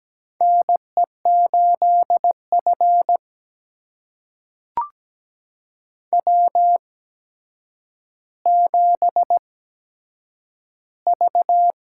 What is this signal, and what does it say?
Morse code, heard as a keyed tone, about 17 wpm, spaced as for 8 wpm, 700 Hz. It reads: NE8F W7V
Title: USA Call Signs - 4 Characters - 8x Character Spacing - 17wpm